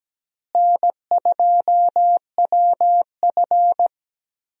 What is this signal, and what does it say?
Morse code, heard as a keyed tone, about 17 wpm, 700 Hz. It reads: N2WF
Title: USA Call Signs - 4 Characters - Repeated 2x - 17wpm